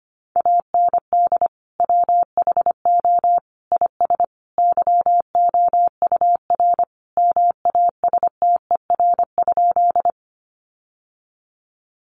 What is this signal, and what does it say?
Morse code, heard as a keyed tone, about 25 wpm, 700 Hz. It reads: AND WHO IS YOUR MASTER?
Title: Sentences from Top 1300 Words 25wpm